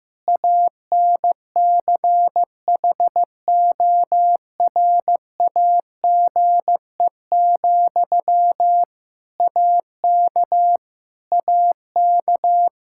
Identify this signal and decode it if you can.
Morse code, heard as a keyed tone, about 15 wpm, 700 Hz. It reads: ANCHORAGE, AK AK
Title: Practice QSO - Exchange 1 - Beginning - 15wpm